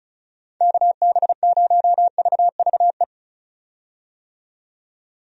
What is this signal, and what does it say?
Morse code, heard as a keyed tone, about 36 wpm, 700 Hz. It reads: KB0VVE
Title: USA Call Signs - 6 Characters - 35wpm